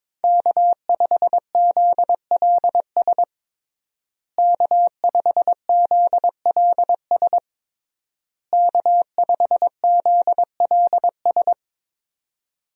Morse code, heard as a keyed tone, about 22 wpm, 700 Hz. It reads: K5ZLS K5ZLS K5ZLS
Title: USA Call Signs - 5 Characters - Repeated 3x - 22wpm